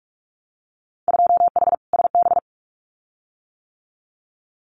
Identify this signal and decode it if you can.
Morse code, heard as a keyed tone, about 46 wpm, 700 Hz. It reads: 2H SB
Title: Winter Field Day Exchanges Encore 45wpm